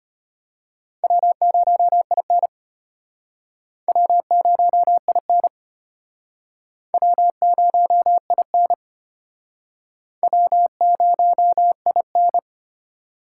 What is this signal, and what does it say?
Morse code, heard as a keyed tone, about 32 wpm, 700 Hz. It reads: W0IN W0IN W0IN W0IN